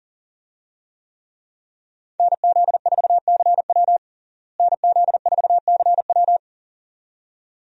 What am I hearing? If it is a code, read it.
Morse code, heard as a keyed tone, about 40 wpm, 700 Hz. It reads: NZ4CW NZ4CW
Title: USA Call Signs - 5 Characters - Repeated 2x - 40wpm